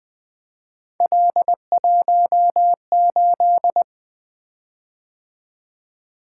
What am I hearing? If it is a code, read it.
Morse code, heard as a keyed tone, about 20 wpm, 700 Hz. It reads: L18
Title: Three Letter-Number Combinations - 20wpm